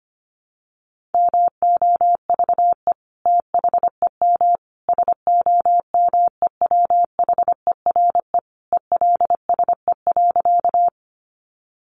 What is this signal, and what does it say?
Morse code, heard as a keyed tone, about 25 wpm, 700 Hz. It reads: MOVE THEM SOMEWHERE ELSE.
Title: Sentences from Top 1400 Words 25wpm